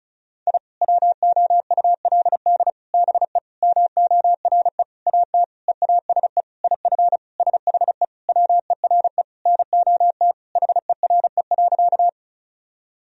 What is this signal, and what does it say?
Morse code, heard as a keyed tone, about 34 wpm, 700 Hz. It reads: I WOULD BE MORE AT EASE IF SHE WERE NOT HERE.
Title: Sentences from Top 1000 Words - Review Part 1 - 35wpm